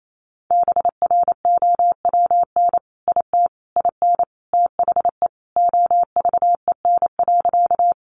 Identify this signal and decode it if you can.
Morse code, heard as a keyed tone, about 28 wpm, 700 Hz. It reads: BROWN IT IN THE OVEN.
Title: Sentences from Top 1200 Words 28wpm